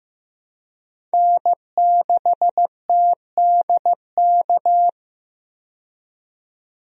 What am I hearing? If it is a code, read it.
Morse code, heard as a keyed tone, about 15 wpm, 700 Hz. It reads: N6TDK
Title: USA Call Signs - 5 Characters - 15wpm